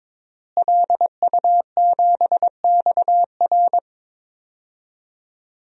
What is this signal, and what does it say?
Morse code, heard as a keyed tone, about 22 wpm, 700 Hz. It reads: LU7XR
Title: DXCC-100 Call Signs - 5 Characters - Repeated 3x - 22wpm